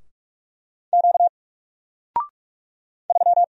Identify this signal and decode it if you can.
Morse code, heard as a keyed tone, about 44 wpm, 700 Hz. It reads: Q 3
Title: Single Letter-Number Alt 45wpm